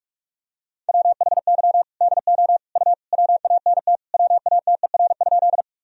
Morse code, heard as a keyed tone, about 46 wpm, 700 Hz. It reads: WHY DO U WANT WATER?
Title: Sentences from Top 200 Words and Abbreviations 45wpm